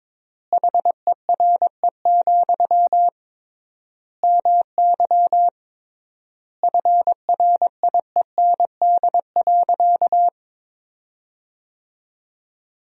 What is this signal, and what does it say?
Morse code, heard as a keyed tone, about 22 wpm, 700 Hz. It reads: HERE, MY FRIEND.
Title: 3x Word-Spacing - Sentences from Top 300 Words - 22wpm